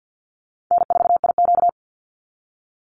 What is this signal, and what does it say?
Morse code, heard as a keyed tone, about 50 wpm, 700 Hz. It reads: N4IQ
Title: Easy Call Signs - 50wpm